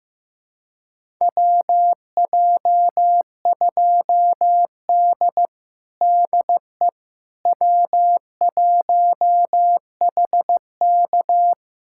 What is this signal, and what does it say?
Morse code, heard as a keyed tone, about 15 wpm, 700 Hz. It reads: WJ2D DE W1HK
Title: USA Callsign DE Callsign - 4 Characters - 15wpm